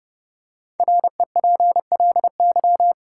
Morse code, heard as a keyed tone, about 30 wpm, 700 Hz. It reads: REPLY
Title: Five Letter Words - 30wpm